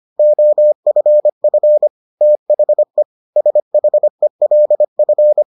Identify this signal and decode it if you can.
Morse code, heard as a keyed tone, about 25 wpm, 600 Hz. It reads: OFF THE SHELF